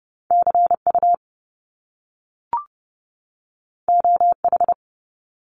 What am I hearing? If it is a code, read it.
Morse code, heard as a keyed tone, about 30 wpm, 700 Hz. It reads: CU OH
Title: Top Two Letter Combinations in Words 30wpm